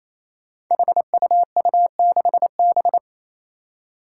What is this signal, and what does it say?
Morse code, heard as a keyed tone, about 28 wpm, 700 Hz. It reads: HUU6B
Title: Code Groups - 5 Letter:Numbers - 28wpm